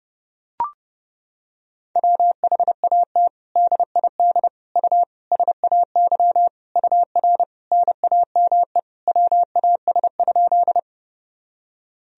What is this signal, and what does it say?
Morse code, heard as a keyed tone, about 32 wpm, 700 Hz. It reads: WHAT DID U SAY UR NAME WAS?
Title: Sentences from Top 200 Words and Abbreviations 30wpm